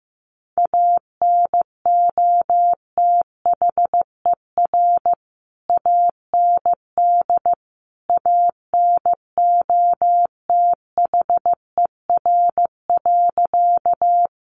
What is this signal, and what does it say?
Morse code, heard as a keyed tone, about 15 wpm, 700 Hz. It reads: ANOTHER AND ANOTHER.
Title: Sentences from Top 600 Words 15wpm